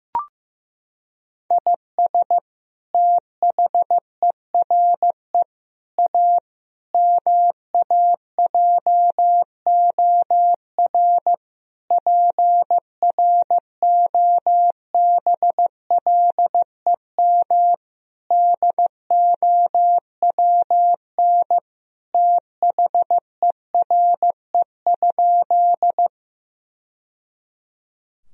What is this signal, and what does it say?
Morse code, heard as a keyed tone, about 15 wpm, 700 Hz. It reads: IS THERE A MAJOR PROBLEM DOWN THERE?
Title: Sentences from Top 1000 Words - Review Part 4 - 15wpm